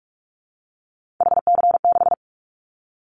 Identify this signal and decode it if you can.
Morse code, heard as a keyed tone, about 44 wpm, 700 Hz. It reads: HC6
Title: Callsign Prefixes for Top 100 DXCC - 45wpm